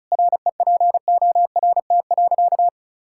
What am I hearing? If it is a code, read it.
Morse code, heard as a keyed tone, about 36 wpm, 700 Hz. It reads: REPORT.
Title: Sentences from Top 1000 Words - Review Part 6 - 35wpm